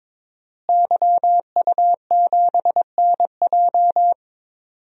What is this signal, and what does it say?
Morse code, heard as a keyed tone, about 22 wpm, 700 Hz. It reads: YU7NJ
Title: DXCC-100 Call Signs - 5 Characters - Repeated 2x - 22wpm